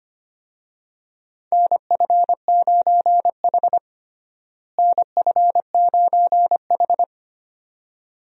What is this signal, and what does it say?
Morse code, heard as a keyed tone, about 25 wpm, 700 Hz. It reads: NF9H NF9H
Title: USA Call Signs - 4 Characters - Repeated 2x - 25wpm